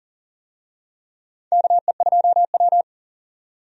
Morse code, heard as a keyed tone, about 40 wpm, 700 Hz. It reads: KE2W
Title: USA Call Signs - 4 Characters - 40wpm